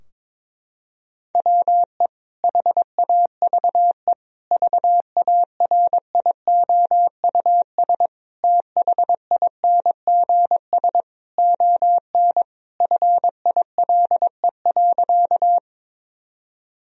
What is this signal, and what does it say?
Morse code, heard as a keyed tone, about 22 wpm, 700 Hz. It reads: WE HAVE VARIOUS THINGS ON FILE.
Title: Sentences from Top 1000 Words - Review Part 4 - 22wpm